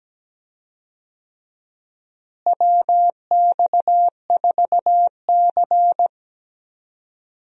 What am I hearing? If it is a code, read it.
Morse code, heard as a keyed tone, about 17 wpm, 700 Hz. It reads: WX4C
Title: USA Call Signs - 4 Characters - Repeated 2x - 17wpm